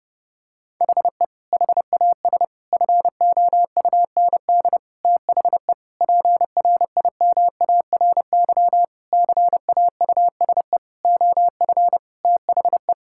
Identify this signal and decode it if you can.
Morse code, heard as a keyed tone, about 32 wpm, 700 Hz. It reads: HE HAS FOUND THE PRIMARY CAUSE OF THE
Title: Sentences from Top 1000 Words - Review Part 5 - 30wpm